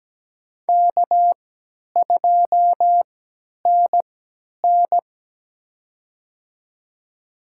Morse code, heard as a keyed tone, about 17 wpm, spaced as for 10 wpm, 700 Hz. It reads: K2NN
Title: USA Call Signs - 4 Characters - 3x Character Spacing - 17wpm